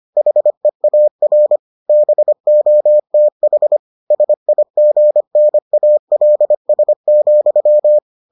Morse code, heard as a keyed tone, about 25 wpm, 600 Hz. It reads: HEAR BOTH SIGNALS,